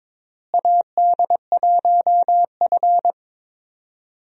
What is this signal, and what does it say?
Morse code, heard as a keyed tone, about 22 wpm, 700 Hz. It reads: AD1F
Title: USA Call Signs - 4 Characters - Repeated 3x - 22wpm